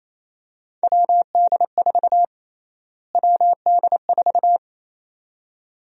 Morse code, heard as a keyed tone, about 28 wpm, 700 Hz. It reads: WD4 WD4
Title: USA Call Sign Prefixes - Repeated 2x - 28wpm